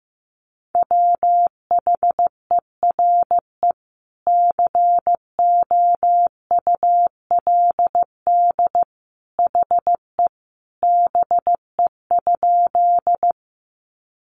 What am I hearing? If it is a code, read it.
Morse code, heard as a keyed tone, about 15 wpm, 700 Hz. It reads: WHERE COULD HE BE?
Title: Sentences from Top 200 Words 15wpm